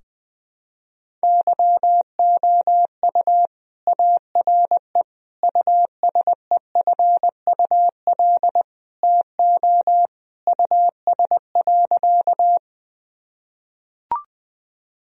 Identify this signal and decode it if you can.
Morse code, heard as a keyed tone, about 20 wpm, 700 Hz. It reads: YOU ARE USEFUL TO US.
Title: Sentences from Top 1000 Words - Review Part 4 - 20wpm